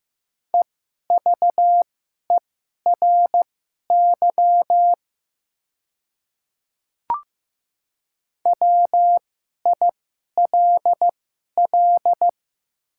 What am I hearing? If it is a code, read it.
Morse code, heard as a keyed tone, about 15 wpm, spaced as for 11 wpm, 700 Hz. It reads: EVERY WILL
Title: 2x Character Spacing - Top 200 Words - 15wpm